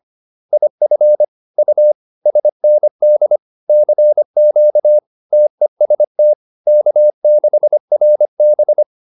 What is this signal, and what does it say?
Morse code, heard as a keyed tone, about 25 wpm, 600 Hz. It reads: IF U SND CQ TEST K6RB